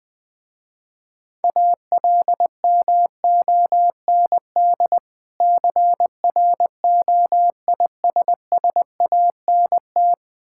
Morse code, heard as a keyed tone, about 20 wpm, 700 Hz. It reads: ALMOND CROISSANT
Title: Difficult Words with Hints - 20wpm